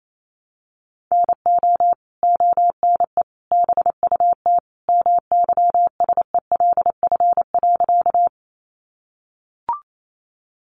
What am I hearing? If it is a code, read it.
Morse code, heard as a keyed tone, about 28 wpm, 700 Hz. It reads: NO ONE BUT MYSELF.
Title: Sentences from Top 1300 Words 28wpm